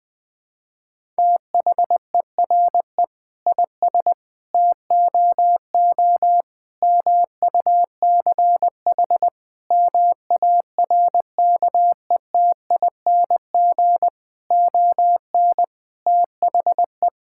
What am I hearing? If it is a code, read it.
Morse code, heard as a keyed tone, about 20 wpm, 700 Hz. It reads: THERE IS TOO MUCH MARKETING ON THE
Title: Sentences from Top 1000 Words - Review Part 5 - 20wpm